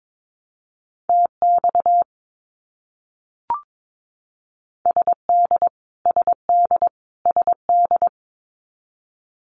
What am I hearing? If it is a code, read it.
Morse code, heard as a keyed tone, about 22 wpm, 700 Hz. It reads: TX SD SD SD
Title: US States Repeated 3x 22wpm